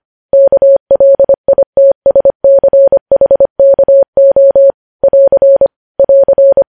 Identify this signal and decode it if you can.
Morse code, heard as a keyed tone, about 25 wpm, 550 Hz. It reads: KLITSCHKO <AR> <AR>
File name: podcast / News headlines-1x-25x25